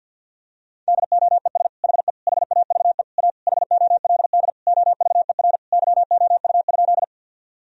Morse code, heard as a keyed tone, about 50 wpm, 700 Hz. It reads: DOES HE HAVE A HOLD OVER YOU?
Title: Sentences from Top 1000 Words - Review Part 2 - 50wpm